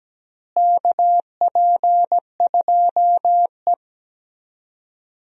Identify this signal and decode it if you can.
Morse code, heard as a keyed tone, about 17 wpm, 700 Hz. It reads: KP2E
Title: DXCC-100 Call Signs - 4 Characters - Repeated 3x - 17wpm